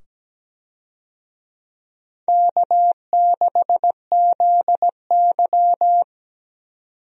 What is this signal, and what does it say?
Morse code, heard as a keyed tone, about 17 wpm, 700 Hz. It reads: K6ZY
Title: USA Call Signs - 4 Characters - 17wpm